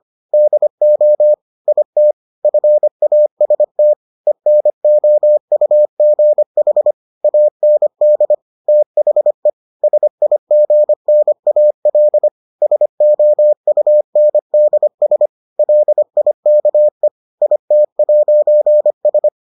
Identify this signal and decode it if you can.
Morse code, heard as a keyed tone, about 25 wpm, 600 Hz. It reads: DO IT FAST ENOUGH AND THE SIGNAL SOUNDS LIKE IT'S